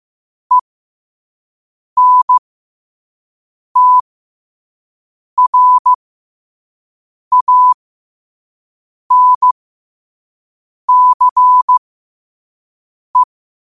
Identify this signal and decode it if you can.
Morse code, heard as a keyed tone, about 15 wpm, spaced as for 5 wpm, 1000 Hz. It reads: ENTRANCE
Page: New quotes every day in morse code at 5 Words per minute.